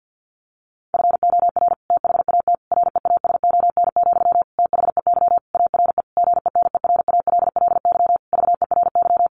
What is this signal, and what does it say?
Morse code, heard as a keyed tone, about 50 wpm, 700 Hz. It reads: FOR THAT REASON, THEY ARE GENERALLY VERY
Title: Sentences from Top 1500 Words 50wpm